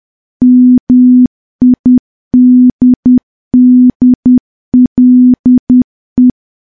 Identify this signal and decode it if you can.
Morse code, heard as a keyed tone, about 10 wpm, 250 Hz. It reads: MIDDLE
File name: morse / morse-training-10WPM